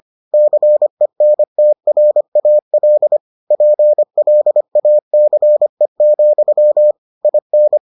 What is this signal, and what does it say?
Morse code, heard as a keyed tone, about 25 wpm, 600 Hz. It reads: CENTRAL PLACE, IN